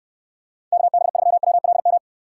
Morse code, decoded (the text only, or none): KB3WLM